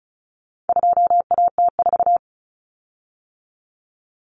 Morse code, heard as a keyed tone, about 34 wpm, 700 Hz. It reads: 2AT4
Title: Common Cut Numbers - 4-Digit 35wpm